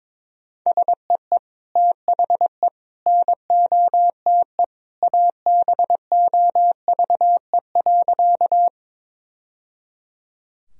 Morse code, heard as a keyed tone, about 22 wpm, 700 Hz. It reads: SEE THE NOTE ABOVE.
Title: Sentences from Top 1000 Words - Review Part 2 - 22wpm